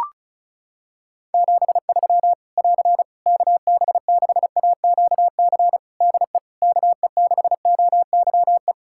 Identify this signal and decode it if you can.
Morse code, heard as a keyed tone, about 36 wpm, 700 Hz. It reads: 73 <AR> KB6AQC DE KE6OYE
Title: Practice QSO - Exchange 4 - Ending - 35wpm